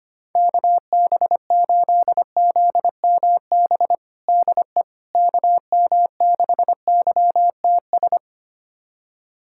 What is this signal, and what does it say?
Morse code, heard as a keyed tone, about 25 wpm, 700 Hz. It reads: KB8ZMB DE KM6YTS
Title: USA Callsign DE Callsign - 6 Characters - 25wpm